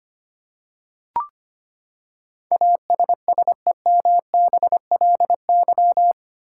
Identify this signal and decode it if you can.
Morse code, heard as a keyed tone, about 25 wpm, 700 Hz. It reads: ASSEMBLY
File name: Eight Letter Words - 25wpm